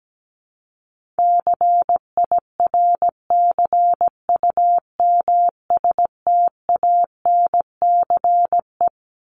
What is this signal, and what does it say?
Morse code, heard as a keyed tone, about 17 wpm, 700 Hz. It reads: CIRCUMSTANCE
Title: ES:S Suffix - Top 100 Words with Base Words 17wpm